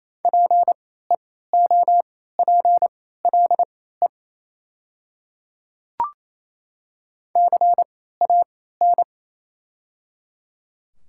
Morse code, heard as a keyed tone, about 28 wpm, spaced as for 16 wpm, 700 Hz. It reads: PEOPLE CAN
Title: 3x Character Spacing - Top 100 Words - 28wpm